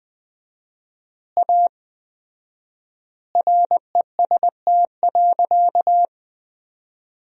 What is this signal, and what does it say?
Morse code, heard as a keyed tone, about 20 wpm, 700 Hz. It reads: A REST.
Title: 4x Word-Spacing - Sentences from Top 500 Words - 20wpm